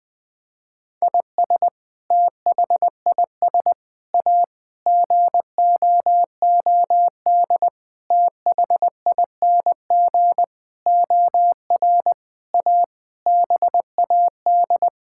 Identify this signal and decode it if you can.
Morse code, heard as a keyed tone, about 20 wpm, 700 Hz. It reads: IS THIS A GOOD THING OR A BAD
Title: Sentences with 7-Word Phrases - Spoken Beginning - 20wpm